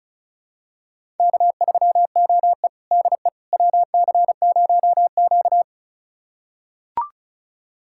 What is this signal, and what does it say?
Morse code, heard as a keyed tone, about 36 wpm, 700 Hz. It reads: K3OE DE WC0Q
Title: USA Callsign DE Callsign - 4 Characters - 35wpm